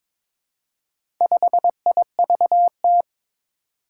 Morse code, heard as a keyed tone, about 21 wpm, 700 Hz. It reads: 5IVT
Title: Four Challenging Letter-Number Combinations 22wpm